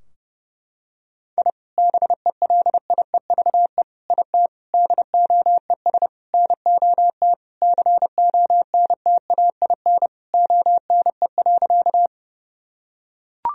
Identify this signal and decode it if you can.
Morse code, heard as a keyed tone, about 30 wpm, 700 Hz. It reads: I BELIEVE IT DOES NOT CONTAIN ONE.
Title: Sentences from Top 1000 Words - Review Part 2 - 30wpm